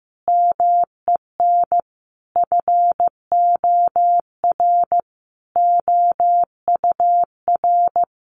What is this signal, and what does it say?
Morse code, heard as a keyed tone, about 15 wpm, 700 Hz. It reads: MEN FOR OUR
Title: Sentences from Top 1400 Words 15wpm